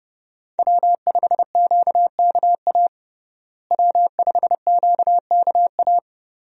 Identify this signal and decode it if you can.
Morse code, heard as a keyed tone, about 30 wpm, 700 Hz. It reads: W5QKA W5QKA
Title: USA Call Signs - 5 Characters - Repeated 2x - 30wpm